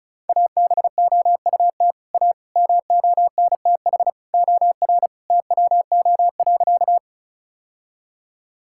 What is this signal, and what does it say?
Morse code, heard as a keyed tone, about 34 wpm, 700 Hz. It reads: ABOUT A MONTH OR TWO.
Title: Sentences from Top 1000 Words - Review Part 5 - 35wpm